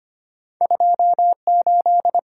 Morse code, heard as a keyed tone, about 25 wpm, 700 Hz. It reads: 28
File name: Two Letter-Number Combinations - 25wpm